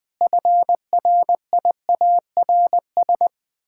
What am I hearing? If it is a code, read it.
Morse code, heard as a keyed tone, about 21 wpm, 700 Hz. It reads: FRIARS
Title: Pangram Challenge v2 - 20wpm